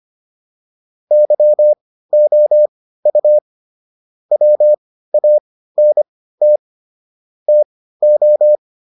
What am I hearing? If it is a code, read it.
Morse code, heard as a keyed tone, about 25 wpm, spaced as for 15 wpm, 600 Hz. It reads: YOU WANT TO